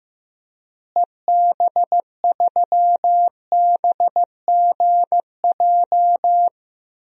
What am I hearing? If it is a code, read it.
Morse code, heard as a keyed tone, about 15 wpm, 700 Hz. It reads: EB3BGJ
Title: DXCC-100 Call Signs - 6 Characters - Repeated 3x - 15wpm